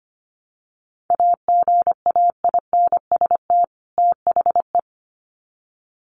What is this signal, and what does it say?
Morse code, heard as a keyed tone, about 25 wpm, 700 Hz. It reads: AGAINST THE
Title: Sets of 2 Words 25wpm